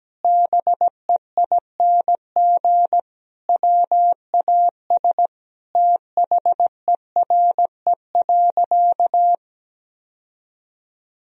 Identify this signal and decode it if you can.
Morse code, heard as a keyed tone, about 17 wpm, 700 Hz. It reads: BEING WAS THERE.
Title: Sentences from Top 1000 Words - Review Part 4 - 17wpm